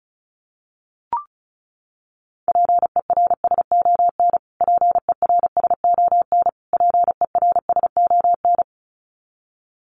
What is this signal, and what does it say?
Morse code, heard as a keyed tone, about 34 wpm, 700 Hz. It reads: PERSON PERSON PERSON
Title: Top 500 Words Repeated 35wpm